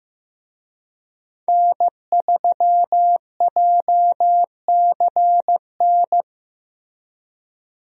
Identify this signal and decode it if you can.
Morse code, heard as a keyed tone, about 15 wpm, 700 Hz. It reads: N3JCN
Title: USA Call Signs - 5 Characters - Repeated 2x - 15wpm